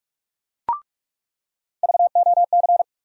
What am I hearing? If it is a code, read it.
Morse code, heard as a keyed tone, about 46 wpm, 700 Hz. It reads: VOC
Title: USA Call Sign Postfixes - 3 Letters 45wpm